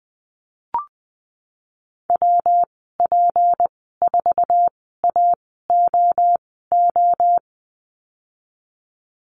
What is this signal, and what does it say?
Morse code, heard as a keyed tone, about 20 wpm, spaced as for 15 wpm, 700 Hz. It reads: WP4AOO
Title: 154 Lesson 25g - Letter p - Call Signs 20wpm